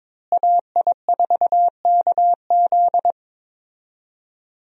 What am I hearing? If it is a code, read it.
Morse code, heard as a keyed tone, about 22 wpm, 690 Hz. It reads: AI4KZ
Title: USA Call Signs - 5 Characters - Repeated 3x - 22wpm